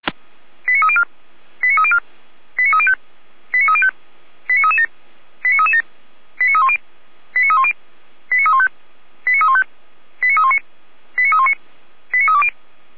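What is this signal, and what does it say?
Five-tone selective call (ZVEI), heard as quick runs of five tones, tones of about 70 ms. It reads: 89304 89304 89305 89305 89337 89337 89310 89310 89315 89315 89319 89319 89320
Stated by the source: Five-tone ZVEI selective call received by SDR.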